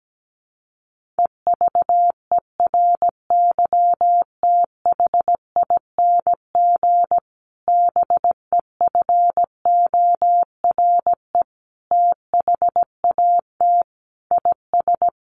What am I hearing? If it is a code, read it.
Morse code, heard as a keyed tone, about 17 wpm, 700 Hz. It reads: EVERYTHING BEFORE THAT IS
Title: Sentences from Top 1500 Words 17wpm